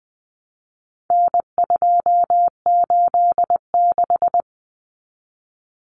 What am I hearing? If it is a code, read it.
Morse code, heard as a keyed tone, about 20 wpm, 700 Hz. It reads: N286
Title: Common Cut Numbers - 4-Digit 20wpm